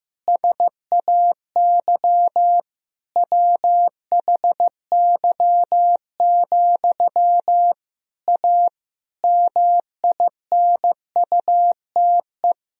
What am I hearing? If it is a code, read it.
Morse code, heard as a keyed tone, about 15 wpm, 700 Hz. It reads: SAY WHY, A MINUTE
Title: Sentences from Top 500 Words and Abbreviations 15wpm